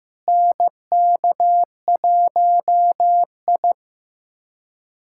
Morse code, heard as a keyed tone, about 15 wpm, 700 Hz. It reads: NK1I
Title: USA Callsign DE Callsign - 4 Characters - 15wpm